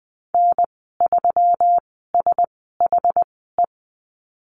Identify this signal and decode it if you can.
Morse code, heard as a keyed tone, about 20 wpm, spaced as for 14 wpm, 700 Hz. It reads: N3SHE
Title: 140 Lesson 23g - Letter f - Call Signs 20wpm